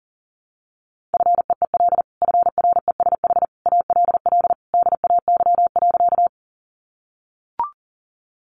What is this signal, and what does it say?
Morse code, heard as a keyed tone, about 40 wpm, 700 Hz. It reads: FEEL FRESH ALL DAY.
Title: Sentences from Top 1200 Words 40wpm